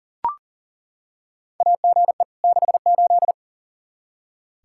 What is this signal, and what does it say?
Morse code, heard as a keyed tone, about 40 wpm, 700 Hz. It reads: AGE 68
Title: Practice QSO - Exchange 3 - More Info - 40wpm